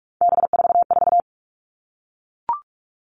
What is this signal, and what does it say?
Morse code, heard as a keyed tone, about 44 wpm, 700 Hz. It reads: B44
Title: All Cut Numbers - 3-Digit 45wpm